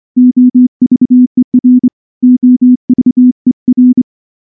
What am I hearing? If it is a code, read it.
Morse code, heard as a keyed tone, about 24 wpm, 250 Hz. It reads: OVER OVER
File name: morse / morse-training-25WPM